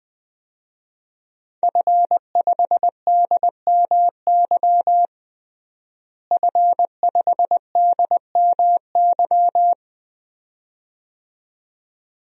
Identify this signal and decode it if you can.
Morse code, heard as a keyed tone, about 20 wpm, 700 Hz. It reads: F5DMY F5DMY
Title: DXCC-100 Call Signs - 5 Characters - Repeated 2x - 20wpm